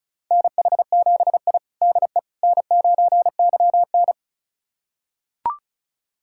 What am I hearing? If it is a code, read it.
Morse code, heard as a keyed tone, about 36 wpm, 700 Hz. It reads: NH7I DE N9YN
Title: USA Callsign DE Callsign - 4 Characters - 35wpm